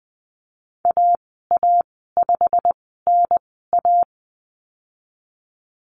Morse code, heard as a keyed tone, about 20 wpm, spaced as for 15 wpm, 700 Hz. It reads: AA5NA
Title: 173 Lesson 28g - Number 9 - Call Signs 20wpm